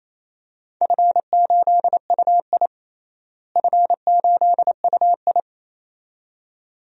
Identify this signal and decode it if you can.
Morse code, heard as a keyed tone, about 28 wpm, 700 Hz. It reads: F8UI F8UI
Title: DXCC-100 Call Signs - 4 Characters - Repeated 2x - 28wpm